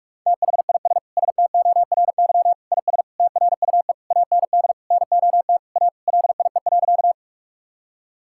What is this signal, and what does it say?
Morse code, heard as a keyed tone, about 43 wpm, 700 Hz. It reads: THIS STORY IS TRUE AND NOT A LIE.
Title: Sentences from Top 1600 Words 45wpm